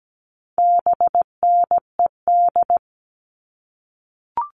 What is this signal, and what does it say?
Morse code, heard as a keyed tone, about 17 wpm, 700 Hz. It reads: BNED
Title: All Cut Numbers - 4-Digit 17wpm